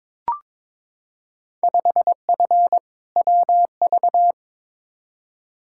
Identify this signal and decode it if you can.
Morse code, heard as a keyed tone, about 22 wpm, 700 Hz. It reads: 5F WV
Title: Field Day Exchanges Encore - 22wpm